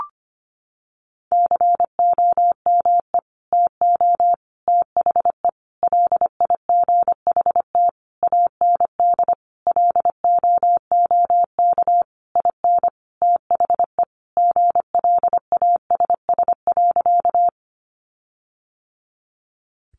Morse code, heard as a keyed tone, about 25 wpm, 700 Hz. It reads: COME TO THE LIGHT AND LOOK IN THE GLASS.
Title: Sentences from Top 1300 Words 25wpm